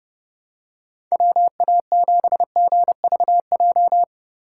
Morse code, heard as a keyed tone, about 31 wpm, 700 Hz. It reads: WA7GVJ
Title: USA Call Signs - 6 Characters - 3x Character Spacing - 30wpm